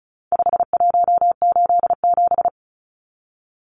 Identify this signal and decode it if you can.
Morse code, heard as a keyed tone, about 34 wpm, 700 Hz. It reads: 5187
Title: Numbers - 4-Digit 35wpm